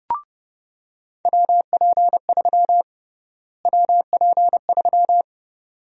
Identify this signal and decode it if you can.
Morse code, heard as a keyed tone, about 32 wpm, 700 Hz. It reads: WP3 WP3
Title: USA Call Sign Prefixes - Repeated 2x - 30wpm